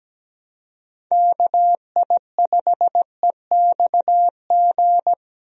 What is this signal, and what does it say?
Morse code, heard as a keyed tone, about 17 wpm, 700 Hz. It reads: KI5EXG
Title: USA Call Signs - 6 Characters - Repeated 2x - 17wpm